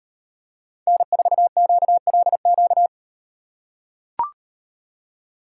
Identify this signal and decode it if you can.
Morse code, heard as a keyed tone, about 38 wpm, 700 Hz. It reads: N4QLQ